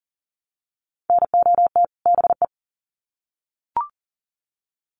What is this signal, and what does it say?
Morse code, heard as a keyed tone, about 40 wpm, 700 Hz. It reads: NOT BE
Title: Sets of 2 Words 40wpm